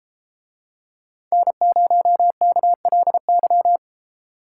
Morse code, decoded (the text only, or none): N0KLY